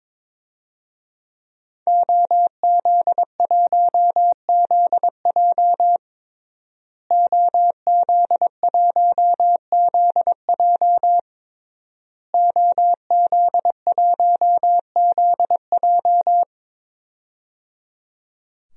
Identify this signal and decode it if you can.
Morse code, heard as a keyed tone, about 22 wpm, 700 Hz. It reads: OZ1ZJ OZ1ZJ OZ1ZJ
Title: DXCC-100 Call Signs - 5 Characters - Repeated 3x - 22wpm